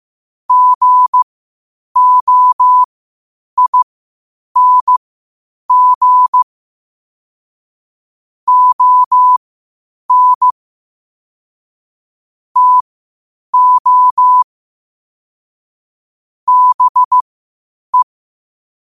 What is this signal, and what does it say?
Morse code, heard as a keyed tone, about 15 wpm, spaced as for 8 wpm, 1000 Hz. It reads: GOING ON TO BE
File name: qotdfiles / 8wpm